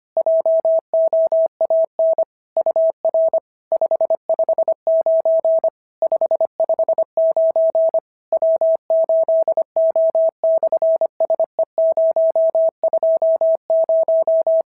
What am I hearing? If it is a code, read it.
Morse code, heard as a keyed tone, about 24 wpm, 650 Hz. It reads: JOAN UR 559 559 W8O/SE020
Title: SOTA QSO Practice 25wpm